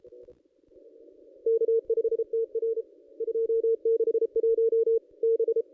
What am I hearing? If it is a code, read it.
Morse code, heard as a keyed tone, about 31 wpm, 470 Hz. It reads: K5TR 261 B
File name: contests / ss-cw-k5tr-op-n5rz -1400